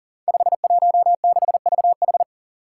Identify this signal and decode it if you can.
Morse code, heard as a keyed tone, about 38 wpm, 700 Hz. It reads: 516VH
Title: Five Challenging Letter-Number Combinations 40wpm